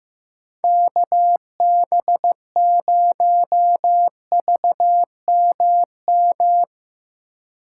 Morse code, heard as a keyed tone, about 16 wpm, 700 Hz. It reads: KB0VMM
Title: Speed Racing - USA Call Signs 15wpm